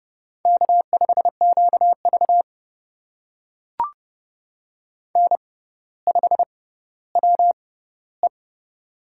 Morse code, heard as a keyed tone, about 31 wpm, spaced as for 15 wpm, 700 Hz. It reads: K5QV N5WE
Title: USA Call Signs - 6x Character Spacing - 30wpm